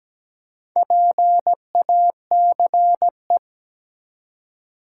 Morse code, heard as a keyed tone, about 17 wpm, 700 Hz. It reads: PACE
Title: Four Letter Words - 17wpm